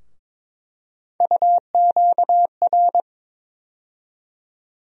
Morse code, heard as a keyed tone, about 22 wpm, 700 Hz. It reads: UQR
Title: USA Call Sign Postfixes - 3 Letters 22wpm